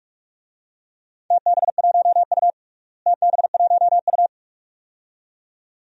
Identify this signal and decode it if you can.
Morse code, heard as a keyed tone, about 46 wpm, 700 Hz. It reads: TB1U TB1U
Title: DXCC-100 Call Signs - 4 Characters - Repeated 2x - 45wpm